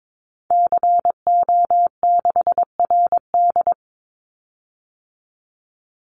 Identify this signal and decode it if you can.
Morse code, heard as a keyed tone, about 22 wpm, 700 Hz. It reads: CO6RD
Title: Call Signs - 22wpm